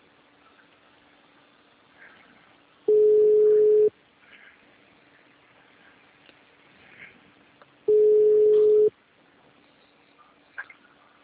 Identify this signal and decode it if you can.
Ringback tone (Europe, 425 Hz).